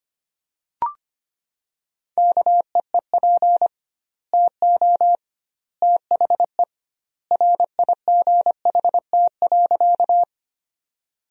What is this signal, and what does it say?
Morse code, heard as a keyed tone, about 25 wpm, 700 Hz. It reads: KEEP TO THE RIGHT.
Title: 2x Word-Spacing - Sentences from Top 300 Words - 25wpm